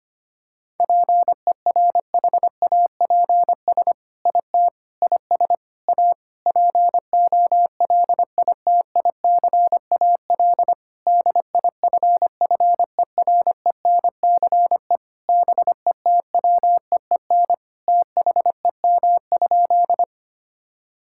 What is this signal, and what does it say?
Morse code, heard as a keyed tone, about 25 wpm, 700 Hz. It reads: PERHAPS IT IS A POLITICAL DIFFERENCE BETWEEN THEM?
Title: Sentences from Top 1000 Words - Review Part 5 - 25wpm